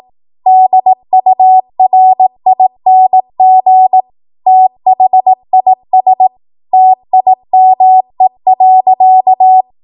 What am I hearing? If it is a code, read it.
Morse code, heard as a keyed tone, about 18 wpm, 750 Hz. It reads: DURING THIS TIME.
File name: Morse / 18WPM